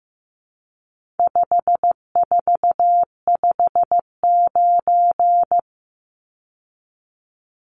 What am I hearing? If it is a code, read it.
Morse code, heard as a keyed tone, about 15 wpm, 700 Hz. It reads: 5459
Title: Numbers - 4-Digit 15wpm